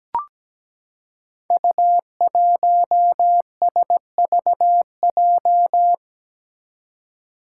Morse code, heard as a keyed tone, about 17 wpm, 700 Hz. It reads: U1SVJ
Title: Five Challenging Letter-Number Combinations 17wpm